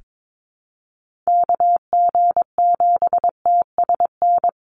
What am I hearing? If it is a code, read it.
Morse code, heard as a keyed tone, about 22 wpm, 700 Hz. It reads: KG7TSN
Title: USA Call Signs 22wpm